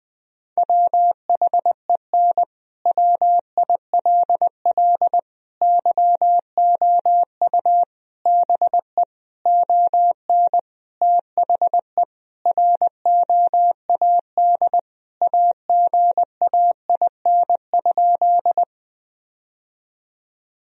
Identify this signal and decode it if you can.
Morse code, heard as a keyed tone, about 20 wpm, 700 Hz. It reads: WHEN WILL YOU BE ON THE ROAD AGAIN?
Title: Sentences from Top 1000 Words - Review Part 2 - 20wpm